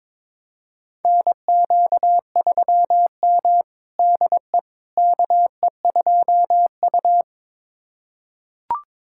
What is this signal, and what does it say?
Morse code, heard as a keyed tone, about 22 wpm, 700 Hz. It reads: NQ3M DE KE2U
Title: USA Callsign DE Callsign - 4 Characters - 22wpm